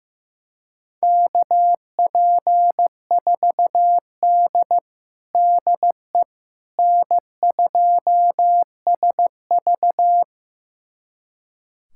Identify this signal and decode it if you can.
Morse code, heard as a keyed tone, about 15 wpm, 700 Hz. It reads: KP4D DE N2SV
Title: USA Callsign DE Callsign - 4 Characters - 15wpm